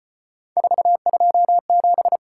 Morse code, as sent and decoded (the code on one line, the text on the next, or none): ....- ..--- --...
427